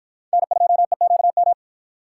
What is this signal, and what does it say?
Morse code, heard as a keyed tone, about 50 wpm, 700 Hz. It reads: N2EQM